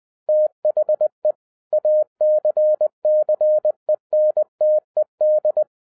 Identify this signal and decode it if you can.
Morse code, heard as a keyed tone, about 20 wpm, 610 Hz. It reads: THE ACCENTED